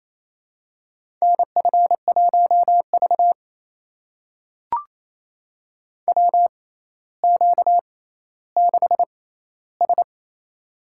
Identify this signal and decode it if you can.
Morse code, heard as a keyed tone, about 28 wpm, spaced as for 14 wpm, 700 Hz. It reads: NF1V WQ6S
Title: USA Call Signs - 4 Characters - 6x Character Spacing - 28wpm